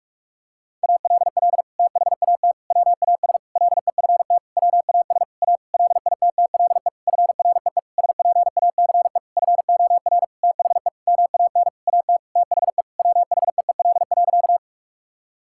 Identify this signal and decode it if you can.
Morse code, heard as a keyed tone, about 46 wpm, 700 Hz. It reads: ALL THAT WAS LEFT WAS A LITTLE FREE SPACE FOR THE MAN AT THE WHEEL.
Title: Sentences from Top 1000 Words - Review Part 5 - 45wpm